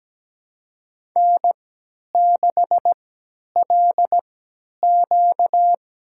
Morse code, heard as a keyed tone, about 17 wpm, spaced as for 10 wpm, 700 Hz. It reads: N6LQ
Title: USA Call Signs - 4 Characters - 3x Character Spacing - 17wpm